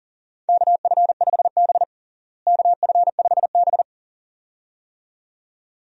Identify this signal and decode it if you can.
Morse code, heard as a keyed tone, about 40 wpm, 700 Hz. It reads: KF5B KF5B
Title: USA Call Signs - 4 Characters - Repeated 2x - 40wpm